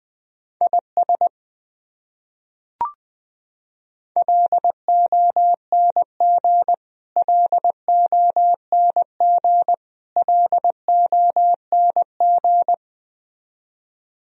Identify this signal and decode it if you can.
Morse code, heard as a keyed tone, about 20 wpm, 700 Hz. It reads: IS LONG LONG LONG
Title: Top 100 Words Repeated 20wpm